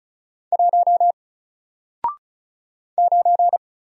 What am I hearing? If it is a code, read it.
Morse code, heard as a keyed tone, about 36 wpm, 700 Hz. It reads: 1 9
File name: Single-Letter-Number-35wpm